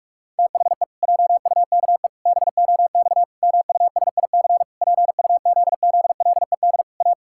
Morse code, heard as a keyed tone, about 46 wpm, 700 Hz. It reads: THE JUKE BOX MUSIC PUZZLED A
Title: Pangram Challenge v2 - 45wpm